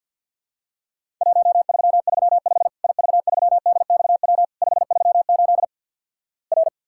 Morse code, heard as a keyed tone, about 52 wpm, 700 Hz. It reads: 1335 IV3DXW 538 R
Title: Worked All Europe DX Contest - QTC Exchange with Repeats - 50wpm